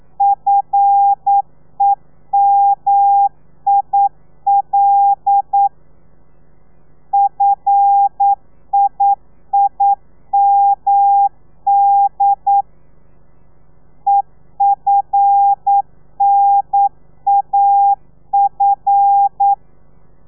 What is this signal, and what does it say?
Morse code, heard as a keyed tone, about 9 wpm, 800 Hz. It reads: FEMIL FIIMD EFNAF
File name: telegrafia / i2tfj / Lez5 40